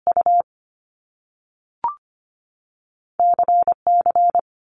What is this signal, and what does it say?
Morse code, heard as a keyed tone, about 25 wpm, spaced as for 3 wpm, 700 Hz. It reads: UC C
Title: Top Two Letter Combinations in Words 25wpm